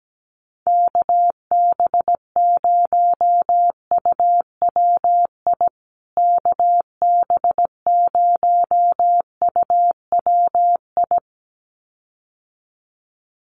Morse code, heard as a keyed tone, about 17 wpm, 700 Hz. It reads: KB0UWI KB0UWI
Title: USA Call Signs Repeated 2x 17wpm